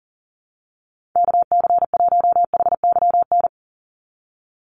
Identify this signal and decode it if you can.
Morse code, heard as a keyed tone, about 40 wpm, 700 Hz. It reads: KC1HYN
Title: USA Call Signs 40wpm